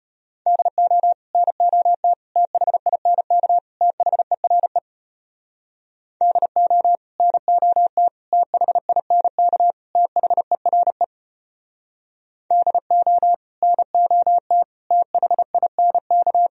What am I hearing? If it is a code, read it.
Morse code, heard as a keyed tone, about 32 wpm, 700 Hz. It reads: DO NOT THINK THERE DO NOT THINK THERE DO NOT THINK